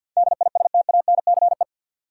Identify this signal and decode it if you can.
Morse code, heard as a keyed tone, about 47 wpm, 700 Hz. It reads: DISTANCE
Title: Sentences from Top 1200 Words - New Word Drill - 50wpm